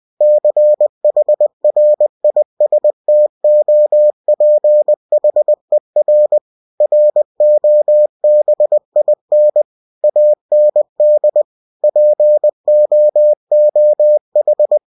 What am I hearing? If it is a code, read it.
Morse code, heard as a keyed tone, about 20 wpm, 600 Hz. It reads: CHRISTOPHER ROBIN AND POOH
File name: cw-books / winnie-the-pooh / winnie-20wpm-page-0101